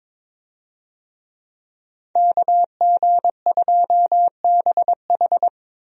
Morse code, heard as a keyed tone, about 22 wpm, 700 Hz. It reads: KG2BH
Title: USA Call Signs - 5 Characters - Repeated 3x - 22wpm